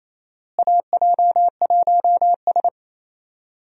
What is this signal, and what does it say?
Morse code, heard as a keyed tone, about 28 wpm, 700 Hz. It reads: AJ1S
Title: USA Call Signs - 4 Characters - Repeated 2x - 28wpm